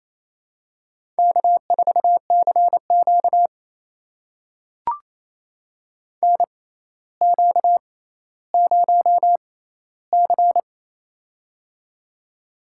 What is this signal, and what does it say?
Morse code, heard as a keyed tone, about 28 wpm, spaced as for 14 wpm, 700 Hz. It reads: K4CQ NQ0C
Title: USA Call Signs - 4 Characters - 6x Character Spacing - 28wpm